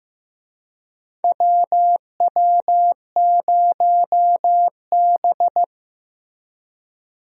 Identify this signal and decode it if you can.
Morse code, heard as a keyed tone, about 15 wpm, 700 Hz. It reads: WW0B
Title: USA Call Signs - 4 Characters - 15wpm